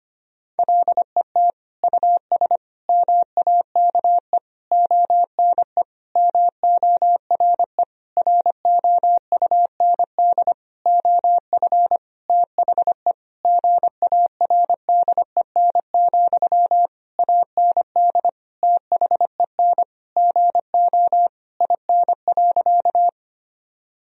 Morse code, heard as a keyed tone, about 25 wpm, 700 Hz. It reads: LET US MAKE ONE MORE ROUND OF THE GARDEN, AND THEN GO IN.
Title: Sentences from Top 1000 Words - Review Part 4 - 25wpm